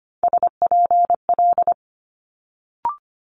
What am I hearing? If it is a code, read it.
Morse code, heard as a keyed tone, about 24 wpm, 700 Hz. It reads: SPL
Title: Multi-letter Phonemes Quick Fire 25wpm